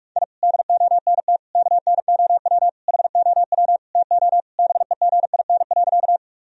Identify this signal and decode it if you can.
Morse code, heard as a keyed tone, about 41 wpm, 700 Hz. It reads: I DONT KNOW HOW TO BEGIN.
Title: Sentences from Top 1000 Words - Review Part 1 - 45wpm